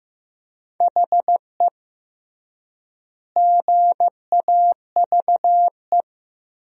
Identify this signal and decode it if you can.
Morse code, heard as a keyed tone, about 16 wpm, 700 Hz. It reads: HE GAVE
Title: 3x Word-Spacing - Sentences from Top 500 Words - 15wpm